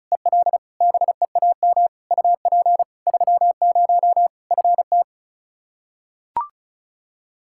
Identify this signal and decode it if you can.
Morse code, heard as a keyed tone, about 36 wpm, 700 Hz. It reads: EL BEAM UP 30 FT
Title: Practice QSO - Exchange 2 - Info - 35wpm